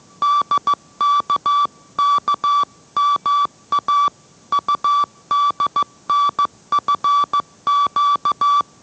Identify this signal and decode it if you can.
Morse code, heard as a keyed tone, about 15 wpm, 1160 Hz. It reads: DKKMA UDNFQ